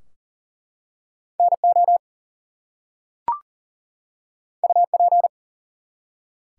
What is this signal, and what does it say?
Morse code, heard as a keyed tone, about 44 wpm, 700 Hz. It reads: NO UP
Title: Two Letter Words - 40wpm